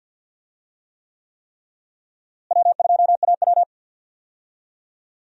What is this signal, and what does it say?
Morse code, heard as a keyed tone, about 46 wpm, 700 Hz. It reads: W2AW
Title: USA Call Signs - 4 Characters - Repeated 3x - 50wpm